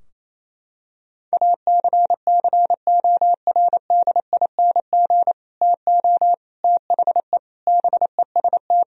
Morse code, heard as a keyed tone, about 28 wpm, 700 Hz. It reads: ACCORDING TO THE BEST
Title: Sentences from Top 1000 Words - Review Part 4 - 28wpm